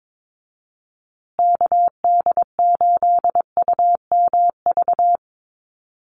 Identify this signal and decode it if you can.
Morse code, heard as a keyed tone, about 22 wpm, 700 Hz. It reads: KD8UMV
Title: USA Call Signs Repeated 2x 22wpm